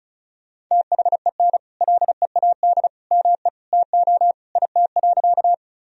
Morse code, heard as a keyed tone, about 36 wpm, 700 Hz. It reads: THEN LEAD ME TO IT.
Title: Sentences from Top 1000 Words - Review Part 2 - 35wpm